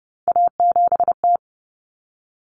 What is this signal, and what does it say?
Morse code, heard as a keyed tone, about 30 wpm, 700 Hz. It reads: A7T
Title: Common Cut Numbers - 3-Digit 30wpm